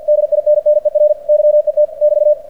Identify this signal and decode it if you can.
Morse code, heard as a keyed tone, about 46 wpm, 600 Hz. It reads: DINNER YET <BT>
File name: archived / CRSnet